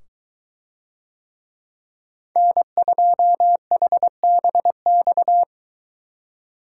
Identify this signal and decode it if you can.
Morse code, heard as a keyed tone, about 23 wpm, 700 Hz. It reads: N2HBX